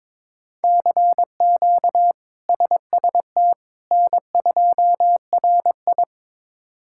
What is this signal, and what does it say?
Morse code, heard as a keyed tone, about 22 wpm, 700 Hz. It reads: CQ SST N2RI
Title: K1USN SST QSO Practice - 22wpm